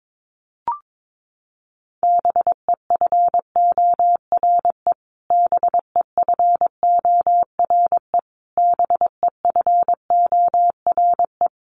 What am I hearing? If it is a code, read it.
Morse code, heard as a keyed tone, about 22 wpm, 700 Hz. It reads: BEFORE BEFORE BEFORE
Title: Top 500 Words Repeated 22wpm